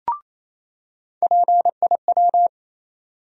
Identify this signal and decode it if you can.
Morse code, heard as a keyed tone, about 28 wpm, 700 Hz. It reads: PIW
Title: USA Call Sign Postfixes - 3 Letters 28wpm